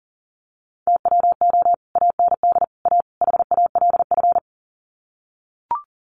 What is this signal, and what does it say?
Morse code, heard as a keyed tone, about 40 wpm, 700 Hz. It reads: TWO AND A HALF
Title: Sets of 4 Words 40wpm